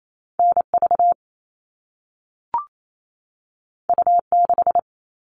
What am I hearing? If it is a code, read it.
Morse code, heard as a keyed tone, about 28 wpm, 700 Hz. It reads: NV U6
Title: All Cut Numbers - 2-Digit 28wpm